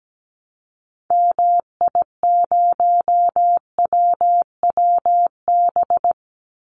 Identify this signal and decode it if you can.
Morse code, heard as a keyed tone, about 17 wpm, 700 Hz. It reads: MI0WWB
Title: Call Signs - 17wpm